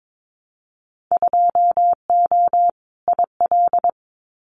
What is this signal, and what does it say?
Morse code, heard as a keyed tone, about 22 wpm, 700 Hz. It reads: 2O IL
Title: Winter Field Day Exchanges 22wpm